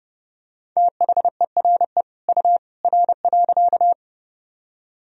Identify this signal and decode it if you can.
Morse code, heard as a keyed tone, about 30 wpm, 700 Hz. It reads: THERE U R.
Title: Sentences from Top 100 Words and Abbreviations 30wpm